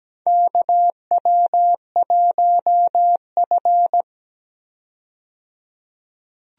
Morse code, heard as a keyed tone, about 17 wpm, 700 Hz. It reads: KW1F
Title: USA Call Signs - 4 Characters - 17wpm